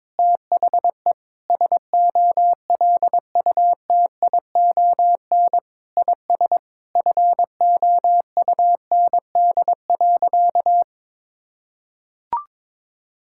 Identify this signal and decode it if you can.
Morse code, heard as a keyed tone, about 22 wpm, 700 Hz. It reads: THE SOLUTION IS FOUND.
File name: Sentences with 9-Word Phrases - 22wpm